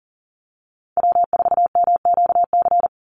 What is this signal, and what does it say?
Morse code, heard as a keyed tone, about 39 wpm, 700 Hz. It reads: W4MQC
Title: Call Signs - 40wpm